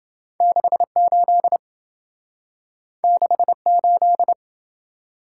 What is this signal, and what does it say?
Morse code, heard as a keyed tone, about 28 wpm, 700 Hz. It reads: 68 68